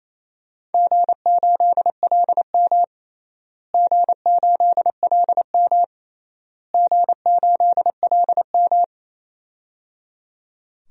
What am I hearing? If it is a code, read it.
Morse code, heard as a keyed tone, about 28 wpm, 700 Hz. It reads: G8LM G8LM G8LM
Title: DXCC-100 Call Signs - 4 Characters - Repeated 3x - 28wpm